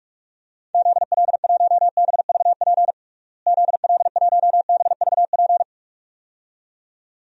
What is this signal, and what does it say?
Morse code, heard as a keyed tone, about 41 wpm, 700 Hz. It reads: ZL1BVP ZL1BVP
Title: DXCC-100 Call Signs - 6 Characters - Repeated 2x - 45wpm